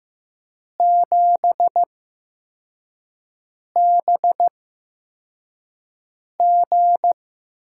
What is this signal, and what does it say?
Morse code, heard as a keyed tone, about 15 wpm, spaced as for 4 wpm, 700 Hz. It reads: 7BG
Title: USA Call Signs - 5 Characters - 8x Character Spacing - 15wpm